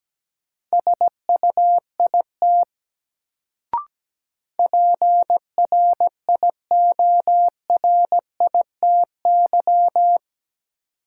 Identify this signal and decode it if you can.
Morse code, heard as a keyed tone, about 17 wpm, 700 Hz. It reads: SUIT PRIORITY
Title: Sentences from Top 1600 Words - New Word Drill Alt - 17wpm